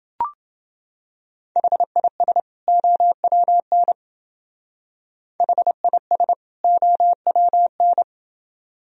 Morse code, heard as a keyed tone, about 30 wpm, 700 Hz. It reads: HIS OWN HIS OWN